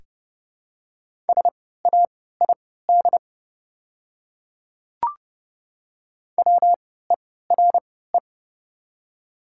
Morse code, heard as a keyed tone, about 30 wpm, spaced as for 17 wpm, 700 Hz. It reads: SAID WERE
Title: 3x Character Spacing - Top 100 Words - 30wpm